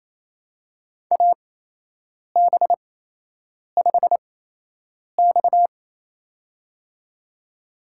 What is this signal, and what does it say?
Morse code, heard as a keyed tone, about 27 wpm, spaced as for 8 wpm, 700 Hz. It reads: AB5X
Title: USA Call Signs - 8x Character Spacing - 28wpm